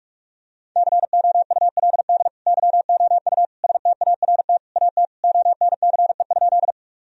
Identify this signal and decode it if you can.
Morse code, heard as a keyed tone, about 46 wpm, 700 Hz. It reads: COULD YOU START AT ONCE?
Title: Sentences from Top 1000 Words - Review Part 1 - 45wpm